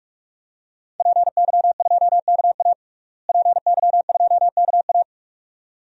Morse code, heard as a keyed tone, about 42 wpm, 700 Hz. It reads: PY2KA PY2KA
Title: DXCC-100 Call Signs - 5 Characters - Repeated 2x - 45wpm